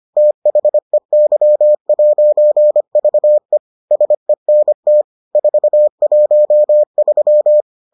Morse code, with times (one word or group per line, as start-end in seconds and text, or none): 0.2-3.6 THEY'VE
3.9-5.0 SENT
5.4-7.6 413